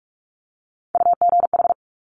Morse code, heard as a keyed tone, about 46 wpm, 700 Hz. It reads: UGH
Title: Top Three Letter Combinations in Words 45wpm